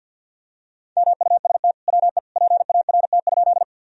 Morse code, heard as a keyed tone, about 51 wpm, 700 Hz. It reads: MUST WE PART?
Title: Sentences from Top 1000 Words - Review Part 1 - 50wpm